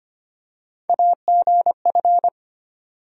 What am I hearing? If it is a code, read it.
Morse code, heard as a keyed tone, about 25 wpm, 700 Hz. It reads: AGF
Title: USA Call Sign Postfixes - 3 Letters 25wpm